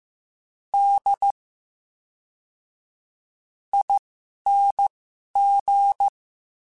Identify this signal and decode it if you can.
Morse code, heard as a keyed tone, about 15 wpm, spaced as for 11 wpm, 800 Hz. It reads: D ING
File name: Special INGs / Special INGs-mp3s / VerySlowToFaster Ing @10